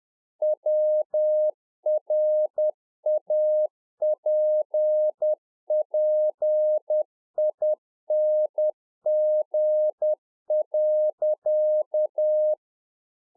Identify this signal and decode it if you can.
Morse code, heard as a keyed tone, about 10 wpm, 610 Hz. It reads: WRAPPING.
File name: Czech Army Morse Keys 10WPM